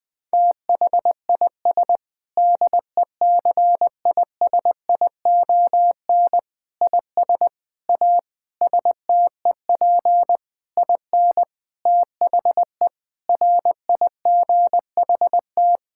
Morse code, heard as a keyed tone, about 20 wpm, 700 Hz. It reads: THIS DECISION IS A STEP IN THE RIGHT
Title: Sentences with 7-Word Phrases - 20wpm